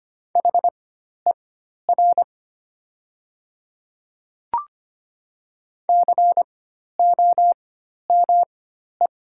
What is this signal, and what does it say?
Morse code, heard as a keyed tone, about 25 wpm, spaced as for 12 wpm, 700 Hz. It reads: HER COME
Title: 4x Character Spacing - Top 100 Words - 25wpm